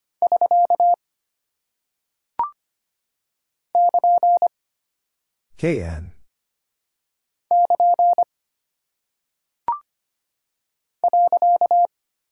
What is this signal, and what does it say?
Morse code, heard as a keyed tone, about 25 wpm, 700 Hz. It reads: <SK> <KN> <KN> .
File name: Single Punctuation-Prosign - 25wpm